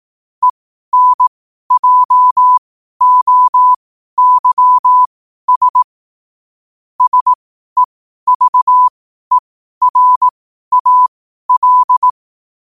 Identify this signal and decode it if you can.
Morse code, heard as a keyed tone, about 18 wpm, spaced as for 12 wpm, 1000 Hz. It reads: ENJOYS SEVERAL
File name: qotdfiles / 12wpm